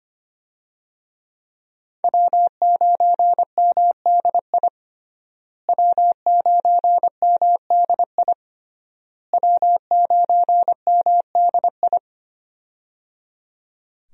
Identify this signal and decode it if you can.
Morse code, heard as a keyed tone, about 25 wpm, 700 Hz. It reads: W9MDI W9MDI W9MDI
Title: USA Call Signs - 5 Characters - Repeated 3x - 25wpm